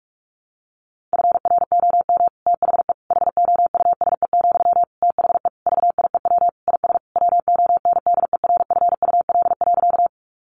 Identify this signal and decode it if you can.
Morse code, heard as a keyed tone, about 44 wpm, 700 Hz. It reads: FROM THE HOUSE, THE VIEW IS WONDERFUL.
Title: Sentences from Top 1100 Words 45wpm